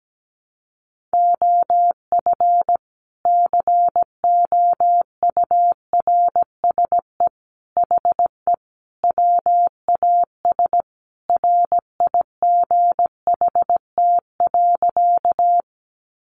Morse code, heard as a keyed tone, about 17 wpm, 700 Hz. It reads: OF COURSE HE WAS RIGHT.
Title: Sentences from Top 500 Words 17wpm